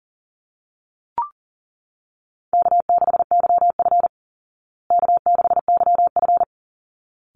Morse code, heard as a keyed tone, about 40 wpm, 700 Hz. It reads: K6YF K6YF
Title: Easy Call Signs Repeated 2x - 40wpm